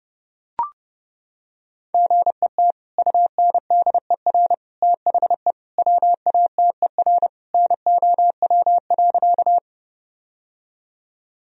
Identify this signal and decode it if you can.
Morse code, heard as a keyed tone, about 30 wpm, 700 Hz. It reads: GET UNDER THE WATER NOW.
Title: Sentences from Top 1000 Words - Review Part 1 - 30wpm